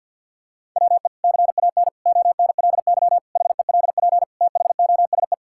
Morse code, heard as a keyed tone, about 51 wpm, 700 Hz. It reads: WE CAN ONLY HELP THOSE
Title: Sentences from Top 1000 Words - Review Part 5 - 50wpm